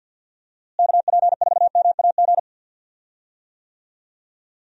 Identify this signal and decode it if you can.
Morse code, heard as a keyed tone, about 52 wpm, 700 Hz. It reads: KP4MAG
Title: USA Call Signs - 6 Characters - Repeated 3x - 50wpm